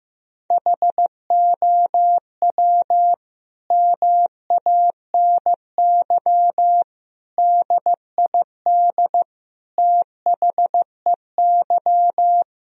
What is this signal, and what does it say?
Morse code, heard as a keyed tone, about 15 wpm, 700 Hz. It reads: HOW MANY DID THEY
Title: Sentences from Top 1000 Words - Review Part 1 - 15wpm